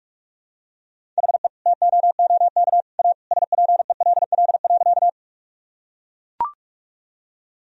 Morse code, heard as a keyed tone, about 41 wpm, 700 Hz. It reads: HE TOOK A SPELL.
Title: Sentences from Top 1000 Words - Review Part 1 - 45wpm